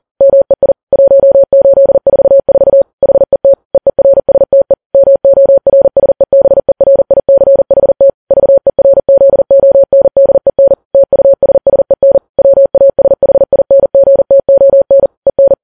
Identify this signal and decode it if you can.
Morse code, heard as a keyed tone, about 40 wpm, 550 Hz. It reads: MEI 1844 HET EERSTE MORSEBERICHT VERZONDEN TUSSEN WASHINGTON EN
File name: Files Web tls / Morse on9fv / morse40kps